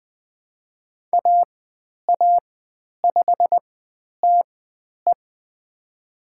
Morse code, heard as a keyed tone, about 20 wpm, spaced as for 10 wpm, 700 Hz. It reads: AA5TE
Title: 102 Lesson 17g - Letter u - Call Signs 20wpm